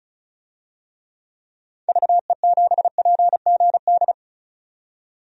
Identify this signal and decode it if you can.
Morse code, heard as a keyed tone, about 34 wpm, 700 Hz. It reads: VE7PGD
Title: DXCC-100 Call Signs - 6 Characters - 35wpm